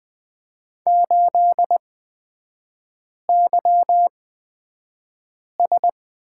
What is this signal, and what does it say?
Morse code, heard as a keyed tone, about 20 wpm, spaced as for 5 wpm, 700 Hz. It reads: 8YS
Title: 236 Lesson 38e - Letter q - Three Characters 20wpm